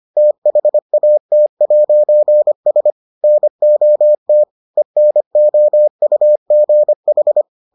Morse code, heard as a keyed tone, about 25 wpm, 600 Hz. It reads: THAT'S NOT ENOUGH